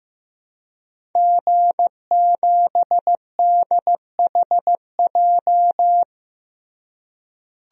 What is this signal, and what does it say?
Morse code, heard as a keyed tone, about 15 wpm, 700 Hz. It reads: G7DHJ
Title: DXCC-100 Call Signs - 5 Characters - Repeated 3x - 15wpm